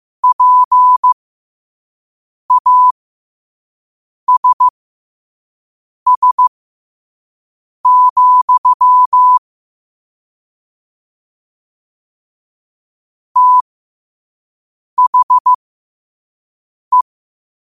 Morse code, heard as a keyed tone, about 15 wpm, spaced as for 5 wpm, 1000 Hz. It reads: PASS, THE
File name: qotdfiles / 5wpm